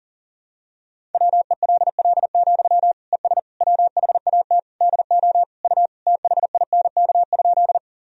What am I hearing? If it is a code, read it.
Morse code, heard as a keyed tone, about 40 wpm, 700 Hz. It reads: WELL, ES WHAT DO U THINK?
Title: Sentences from Top 200 Words and Abbreviations 40wpm